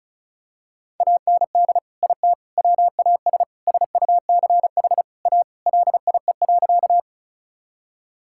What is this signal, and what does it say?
Morse code, heard as a keyed tone, about 36 wpm, 700 Hz. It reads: AND IT WAS SUCH A LIE.
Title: Sentences from Top 1600 Words 35wpm